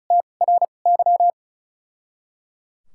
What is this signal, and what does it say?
Morse code, heard as a keyed tone, about 35 wpm, spaced as for 25 wpm, 700 Hz. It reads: TRY
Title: 2x Character Spacing - Top 300 Words - 35wpm